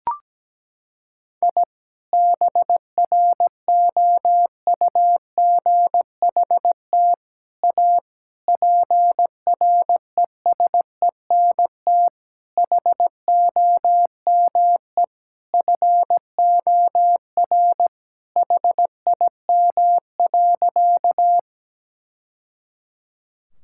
Morse code, heard as a keyed tone, about 17 wpm, 700 Hz. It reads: I BROUGHT A PRESENT HOME FOR HIM.
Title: Sentences from Top 1000 Words - Review Part 5 - 17wpm